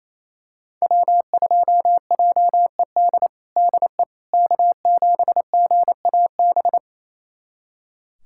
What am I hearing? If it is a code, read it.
Morse code, heard as a keyed tone, about 28 wpm, 700 Hz. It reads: W2JED DE K7GAB
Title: USA Callsign DE Callsign - 5 Characters - 28wpm